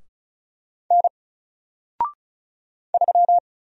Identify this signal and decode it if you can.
Morse code, heard as a keyed tone, about 36 wpm, 700 Hz. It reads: N 3
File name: Single-Letter-Number-35wpm